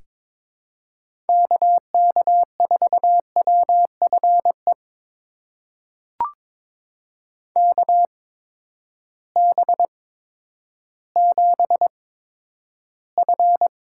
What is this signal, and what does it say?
Morse code, heard as a keyed tone, about 22 wpm, spaced as for 11 wpm, 700 Hz. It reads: KK4WFE KB7F
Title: USA Call Signs - 6 Characters - 8x Character Spacing - 22wpm